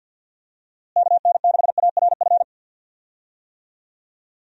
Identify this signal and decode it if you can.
Morse code, heard as a keyed tone, about 50 wpm, 700 Hz. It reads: KN6ARF
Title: USA Call Signs - 6 Characters - Repeated 3x - 50wpm